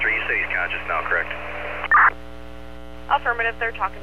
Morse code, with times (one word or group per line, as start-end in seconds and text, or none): none